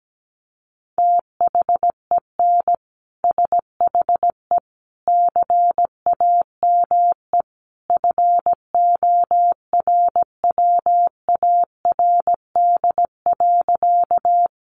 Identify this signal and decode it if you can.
Morse code, heard as a keyed tone, about 17 wpm, 700 Hz. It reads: THEN SHE CAME FORWARD.
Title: Sentences from Top 1300 Words 17wpm